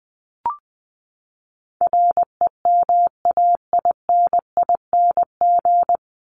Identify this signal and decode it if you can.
Morse code, heard as a keyed tone, about 20 wpm, 700 Hz. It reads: REMAINING
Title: ING Suffix - Top 100 Words with Base Words 20wpm